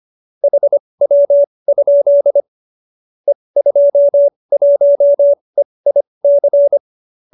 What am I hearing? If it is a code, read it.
Morse code, heard as a keyed tone, about 25 wpm, spaced as for 20 wpm, 570 Hz. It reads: HW? E21EIC
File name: qso208b 20